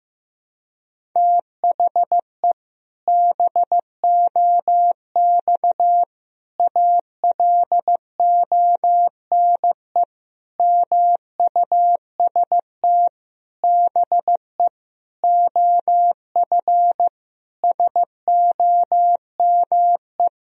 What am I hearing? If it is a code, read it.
Morse code, heard as a keyed tone, about 15 wpm, 700 Hz. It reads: THE BOX ALONE MUST BE OF SOME
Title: Sentences from Top 1000 Words - Review Part 3 - 15wpm